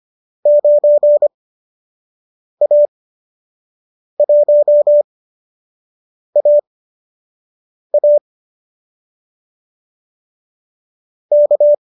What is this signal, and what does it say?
Morse code, heard as a keyed tone, about 25 wpm, spaced as for 6 wpm, 590 Hz. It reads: 9A1AA K
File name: qso012 06